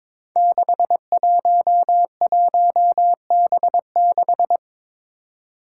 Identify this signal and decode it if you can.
Morse code, heard as a keyed tone, about 21 wpm, 700 Hz. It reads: 611B6
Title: Five Challenging Letter-Number Combinations 22wpm